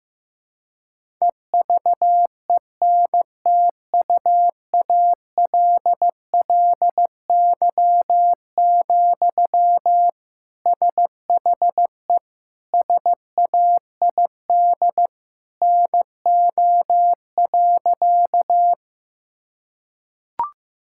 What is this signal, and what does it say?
Morse code, heard as a keyed tone, about 15 wpm, 700 Hz. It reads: EVENTUALLY, SHE SAID NO.
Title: Sentences from Top 1000 Words - Review Part 6 - 15wpm